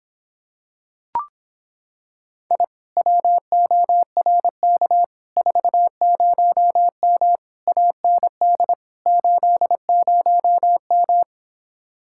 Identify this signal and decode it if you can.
Morse code, heard as a keyed tone, about 26 wpm, 700 Hz. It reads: I WORK 40M AND 80M